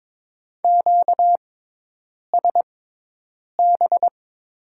Morse code, heard as a keyed tone, about 21 wpm, spaced as for 8 wpm, 700 Hz. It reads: QSB
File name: Top 90 QSO Elements - 6x Character Spacing - 22wpm